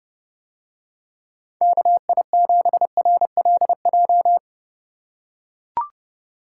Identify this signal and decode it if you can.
Morse code, heard as a keyed tone, about 30 wpm, 700 Hz. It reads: KI7RLJ